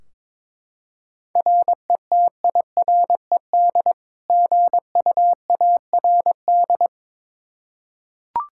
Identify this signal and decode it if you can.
Morse code, heard as a keyed tone, about 23 wpm, 700 Hz. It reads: RETIRED GUARD
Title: K1USN SST QSO Practice 22wpm